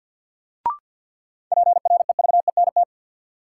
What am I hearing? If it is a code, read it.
Morse code, heard as a keyed tone, about 53 wpm, 700 Hz. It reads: PREVENT
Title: Sentences from Top 1100 Words - New Word Drill Alt - 50wpm